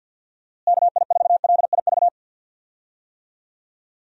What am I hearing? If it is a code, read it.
Morse code, heard as a keyed tone, about 52 wpm, 700 Hz. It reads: KI4LIV
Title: USA Call Signs - 6 Characters - Repeated 3x - 50wpm